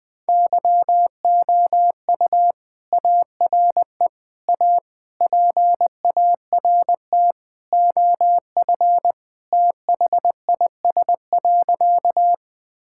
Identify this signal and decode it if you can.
Morse code, heard as a keyed tone, about 20 wpm, 700 Hz. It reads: YOU ARE A PART OF THIS.
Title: Sentences with 3-Word Phrases - 20wpm